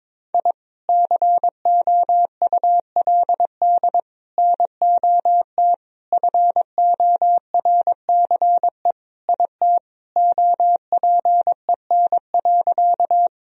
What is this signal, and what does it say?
Morse code, heard as a keyed tone, about 22 wpm, 700 Hz. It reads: I COULD NOT FORCE IT OPEN.
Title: Sentences from Top 1000 Words - Review Part 3 - 22wpm